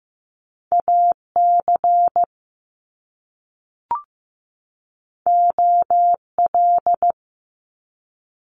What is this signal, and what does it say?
Morse code, heard as a keyed tone, about 15 wpm, 700 Hz. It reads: AC OL
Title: Top Two Letter Combinations in Words 15wpm